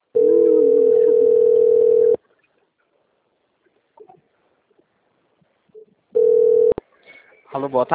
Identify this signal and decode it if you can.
Ringback tone (North America), 440 + 480 Hz.